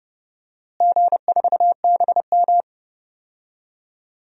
Morse code, heard as a keyed tone, about 30 wpm, 700 Hz. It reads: G4BM
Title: DXCC-100 Call Signs - 4 Characters - Repeated 2x - 30wpm